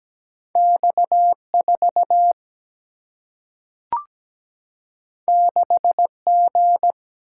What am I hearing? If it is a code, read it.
Morse code, heard as a keyed tone, about 17 wpm, 700 Hz. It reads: X4 6G
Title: Single-Letter Single-Number 17wpm